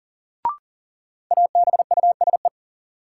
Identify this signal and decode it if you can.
Morse code, heard as a keyed tone, about 40 wpm, 700 Hz. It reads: ABUSE
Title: Sentences from Top 1300 Words - New Word Drill - 40wpm